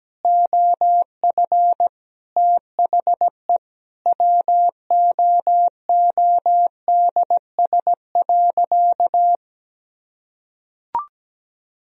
Morse code, heard as a keyed tone, about 17 wpm, 700 Hz. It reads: OF THE WOODS.
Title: Sentences from Top 1000 Words - Review Part 2 - 17wpm